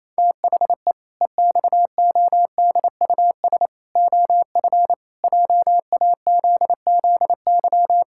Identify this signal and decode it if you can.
Morse code, heard as a keyed tone, about 27 wpm, 700 Hz. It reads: THE EXODUS OF JAZZY
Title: Pangram Challenge v2 - 28wpm